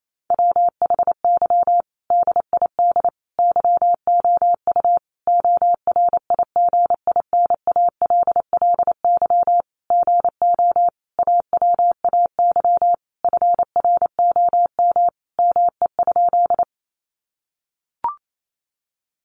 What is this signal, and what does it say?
Morse code, heard as a keyed tone, about 28 wpm, 700 Hz. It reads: WHY DID YOU ORIGINALLY GO AWAY FROM ME?
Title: Sentences from Top 1500 Words 28wpm